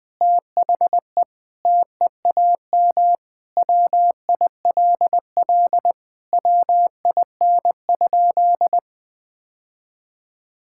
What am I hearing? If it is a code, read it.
Morse code, heard as a keyed tone, about 21 wpm, 700 Hz. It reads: THE TEAM WILL WIN?
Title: Sentences with 4-Word Phrases - 20wpm